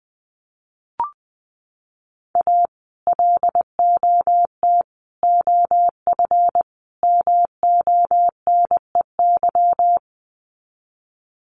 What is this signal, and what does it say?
Morse code, heard as a keyed tone, about 20 wpm, 700 Hz. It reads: A LOT OF MONEY
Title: Sets of 4 Words 20wpm